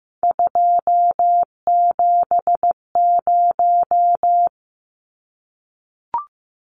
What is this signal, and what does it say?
Morse code, heard as a keyed tone, about 15 wpm, 700 Hz. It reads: 270
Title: Numbers - 3-Digit 15wpm